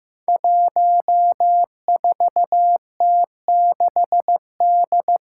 Morse code, heard as a keyed tone, about 15 wpm, 700 Hz. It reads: 14T6D
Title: Five Challenging Letter-Number Combinations 15wpm